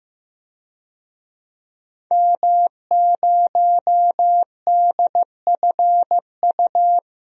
Morse code, heard as a keyed tone, about 15 wpm, 700 Hz. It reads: M0DFU
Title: DXCC-100 Call Signs - 5 Characters - 15wpm